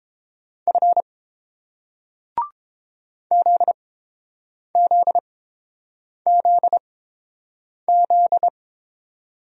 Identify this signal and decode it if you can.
Morse code, heard as a keyed tone, about 31 wpm, spaced as for 7 wpm, 700 Hz. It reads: F ZZZZ